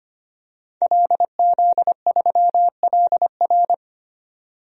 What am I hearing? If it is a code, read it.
Morse code, heard as a keyed tone, about 25 wpm, 700 Hz. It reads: LZ3LR
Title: DXCC-100 Call Signs - 5 Characters - 25wpm